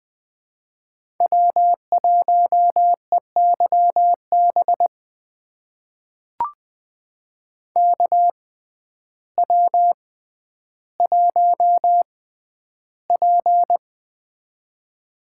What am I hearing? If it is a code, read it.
Morse code, heard as a keyed tone, about 20 wpm, spaced as for 11 wpm, 700 Hz. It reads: W1EYB KW1P
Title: USA Call Signs - 6x Character Spacing - 20wpm